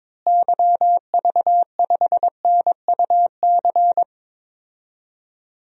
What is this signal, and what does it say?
Morse code, heard as a keyed tone, about 22 wpm, 700 Hz. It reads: YV5NUC
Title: DXCC-100 Call Signs - 6 Characters - Repeated 2x - 22wpm